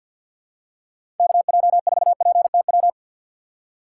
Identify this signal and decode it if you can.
Morse code, heard as a keyed tone, about 52 wpm, 700 Hz. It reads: KJ4PTW
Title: USA Call Signs - 6 Characters - 3x Character Spacing - 50wpm